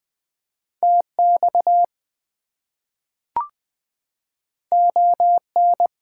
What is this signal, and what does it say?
Morse code, heard as a keyed tone, about 20 wpm, 700 Hz. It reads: TX ON
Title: US States and Canadian Provinces 20wpm